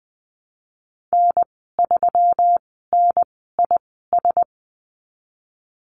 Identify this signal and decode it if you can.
Morse code, heard as a keyed tone, about 20 wpm, spaced as for 15 wpm, 700 Hz. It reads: N3NIS
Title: 186 Lesson 30g - Letter b - Call Signs 20wpm